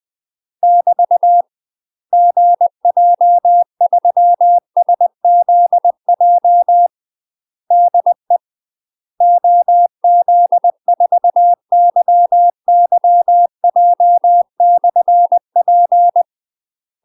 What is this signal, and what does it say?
Morse code, heard as a keyed tone, about 20 wpm, 700 Hz. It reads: <BT> GJ3SZJ DE OZ4YYJ/P
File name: UK MASTER QSOs TRAINING / 20WPM / UK MASTER QSOs 0027